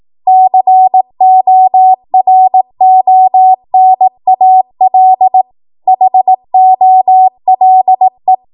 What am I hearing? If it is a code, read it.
Morse code, heard as a keyed tone, about 18 wpm, 750 Hz. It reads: CORONAL HOLE
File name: Morse / 18WPM